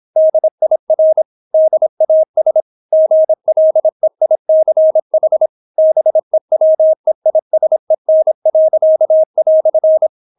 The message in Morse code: DIR DAS GLEICH BEWEISEN."

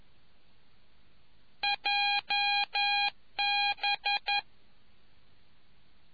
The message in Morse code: JB

